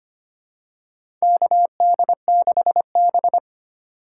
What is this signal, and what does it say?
Morse code, heard as a keyed tone, about 25 wpm, 700 Hz. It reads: KD6B
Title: USA Call Signs - 3x Character Spacing - 25wpm